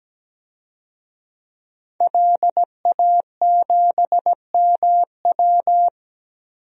Morse code, heard as a keyed tone, about 17 wpm, 700 Hz. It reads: LA7MW
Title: DXCC-100 Call Signs - 5 Characters - 17wpm